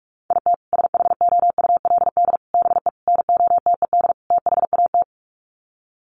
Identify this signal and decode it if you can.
Morse code, heard as a keyed tone, about 44 wpm, 700 Hz. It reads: IT SHOULD BE NOTED THAT
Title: Sets of 5 Words 45wpm